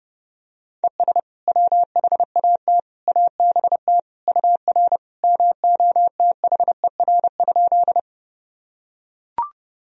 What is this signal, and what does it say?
Morse code, heard as a keyed tone, about 30 wpm, 700 Hz. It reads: ES WHAT ABT UR MOTHER?
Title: Sentences from Top 200 Words and Abbreviations 30wpm